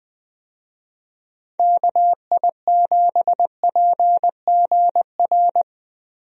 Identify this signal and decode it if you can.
Morse code, heard as a keyed tone, about 20 wpm, 700 Hz. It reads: KI7PGR
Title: USA Call Signs - 6 Characters - Repeated 3x - 20wpm